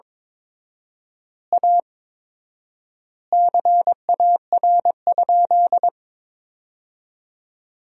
Morse code, heard as a keyed tone, about 22 wpm, 700 Hz. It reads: A CAR?
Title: 4x Word-Spacing - Sentences from Top 300 Words - 22wpm